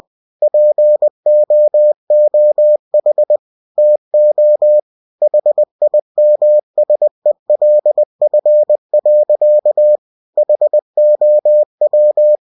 Morse code, heard as a keyed tone, about 20 wpm, 600 Hz. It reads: POOH TO HIMSELF. HOW
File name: cw-books / winnie-the-pooh / winnie-20wpm-page-0103